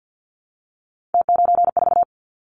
Morse code, heard as a keyed tone, about 50 wpm, 700 Hz. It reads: T94
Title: Callsign Prefixes for Top 100 DXCC - 50wpm